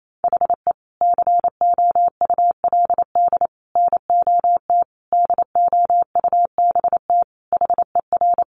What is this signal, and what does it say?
Morse code, heard as a keyed tone, about 28 wpm, 700 Hz. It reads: HE COULD NOT DOUBT HER
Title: Sentences from Top 1500 Words 28wpm